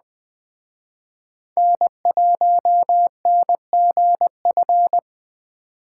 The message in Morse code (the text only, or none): N1NGF